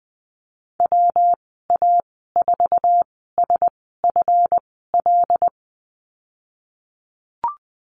Morse code, heard as a keyed tone, about 20 wpm, spaced as for 15 wpm, 700 Hz. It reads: WA4SFL
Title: 179 Lesson 29g - Slash - Call Signs 20wpm